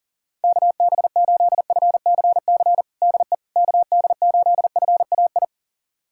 Morse code, heard as a keyed tone, about 40 wpm, 700 Hz. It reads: KB8FCC DE KD8FAI
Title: USA Callsign DE Callsign 40wpm